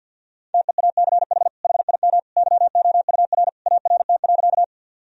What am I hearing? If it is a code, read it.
Morse code, heard as a keyed tone, about 51 wpm, 700 Hz. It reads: TEACH HIM YOUR ART.
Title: Sentences from Top 1000 Words - Review Part 3 - 50wpm